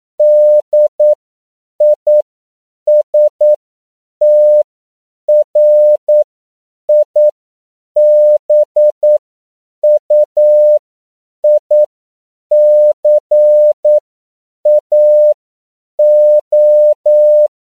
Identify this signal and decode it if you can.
Morse code, heard as a keyed tone, about 9 wpm, spaced as for 7 wpm, 600 Hz. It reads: DISTRIBUICAO
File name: CW / 10PPM